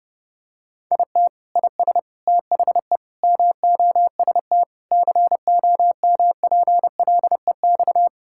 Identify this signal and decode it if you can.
Morse code, heard as a keyed tone, about 32 wpm, 700 Hz. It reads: IT IS THE MOST COMPLEX
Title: Sentences from Top 1000 Words - Review Part 6 - 30wpm